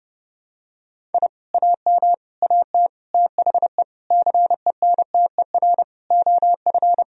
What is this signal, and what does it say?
Morse code, heard as a keyed tone, about 32 wpm, 700 Hz. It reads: I AM AT THE CENTER OF
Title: Sentences from Top 1000 Words - Review Part 2 - 30wpm